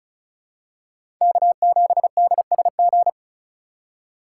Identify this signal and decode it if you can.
Morse code, heard as a keyed tone, about 36 wpm, 700 Hz. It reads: K7DSG
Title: USA Call Signs - 5 Characters - 4x Character Spacing - 35wpm